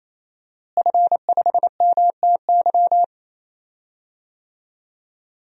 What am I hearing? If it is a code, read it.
Morse code, heard as a keyed tone, about 28 wpm, 700 Hz. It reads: F5MTY
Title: DXCC-100 Call Signs - 5 Characters - Repeated 3x - 28wpm